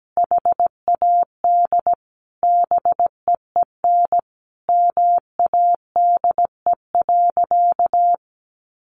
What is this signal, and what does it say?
Morse code, heard as a keyed tone, about 17 wpm, 700 Hz. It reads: HAD BEEN MADE.
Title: Sentences from Top 1400 Words 17wpm